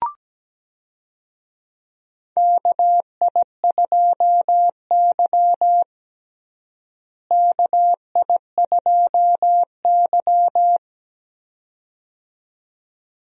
Morse code, heard as a keyed tone, about 17 wpm, 700 Hz. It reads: KI2Y KI2Y
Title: USA Call Signs - 4 Characters - Repeated 2x - 17wpm